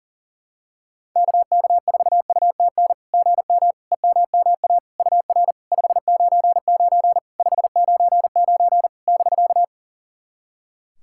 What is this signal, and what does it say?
Morse code, heard as a keyed tone, about 43 wpm, 700 Hz. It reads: KK4UTN GM EMMA UR 599 599 <BK>
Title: SOTA QSO Practice 40wpm